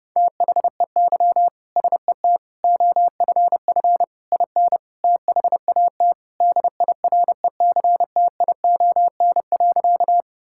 Morse code, heard as a keyed tone, about 30 wpm, 700 Hz. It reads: THEY SET OFF IN THAT DIRECTION.
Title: Sentences from Top 1000 Words - Review Part 5 - 30wpm